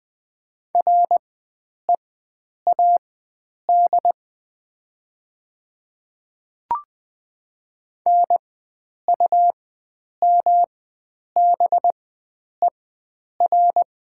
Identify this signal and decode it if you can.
Morse code, heard as a keyed tone, about 20 wpm, spaced as for 9 wpm, 700 Hz. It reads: READ NUMBER
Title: 4x Character Spacing - Top 300 Words - 20wpm